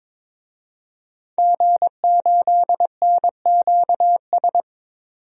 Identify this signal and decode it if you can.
Morse code, heard as a keyed tone, about 22 wpm, 700 Hz. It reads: G8NQS
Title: DXCC-100 Call Signs - 5 Characters - 22wpm